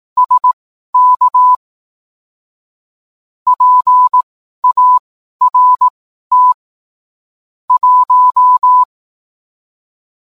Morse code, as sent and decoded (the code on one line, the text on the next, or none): ... -.- / .--. .- .-. - / .----
SK PART 1